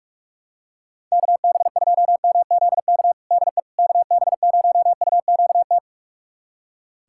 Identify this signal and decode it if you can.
Morse code, heard as a keyed tone, about 45 wpm, 700 Hz. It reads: KB2MZK DE KB0UQT
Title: USA Callsign DE Callsign 45wpm